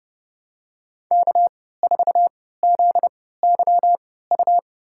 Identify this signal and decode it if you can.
Morse code, heard as a keyed tone, about 31 wpm, spaced as for 17 wpm, 700 Hz. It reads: K4ZYU
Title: USA Call Signs - 5 Characters - 3x Character Spacing - 30wpm